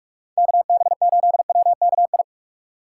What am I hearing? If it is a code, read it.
Morse code, heard as a keyed tone, about 42 wpm, 700 Hz. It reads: KB8WKI
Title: USA Call Signs - 6 Characters - 2x Character Spacing - 45wpm